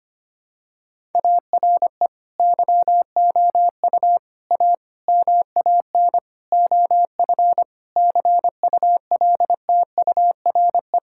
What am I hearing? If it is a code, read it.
Morse code, heard as a keyed tone, about 25 wpm, 700 Hz. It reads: ARE YOU A MAN OF CULTURE
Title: Sentences from Top 1000 Words - Review Part 6 - 25wpm